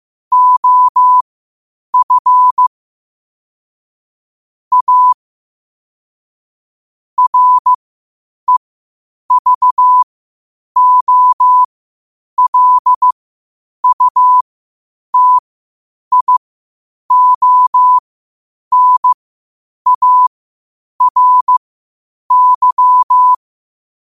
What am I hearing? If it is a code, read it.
Morse code, heard as a keyed tone, about 15 wpm, spaced as for 8 wpm, 1000 Hz. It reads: OF A REVOLUTIONARY